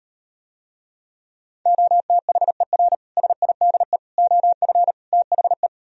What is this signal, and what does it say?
Morse code, heard as a keyed tone, about 38 wpm, 700 Hz. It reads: OTHER SIDE OF THE